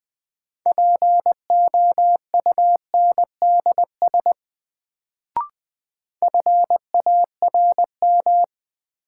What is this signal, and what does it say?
Morse code, heard as a keyed tone, about 20 wpm, 700 Hz. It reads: POUNDS FARM
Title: Sentences from Top 1000 Words - Review Part 2 - New Word Drill - 20wpm